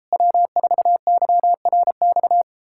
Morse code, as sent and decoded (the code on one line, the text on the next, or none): .-- ....- -.-- .-. -..-
W4YRX